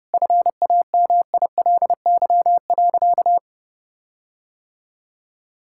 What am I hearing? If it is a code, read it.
Morse code, heard as a keyed tone, about 30 wpm, 700 Hz. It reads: FAMILY.
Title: Sentences from Top 1600 Words 30wpm